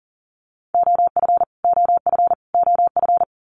Morse code, heard as a keyed tone, about 40 wpm, 700 Hz. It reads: OF OF OF
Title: Top 200 Words Repeated 40wpm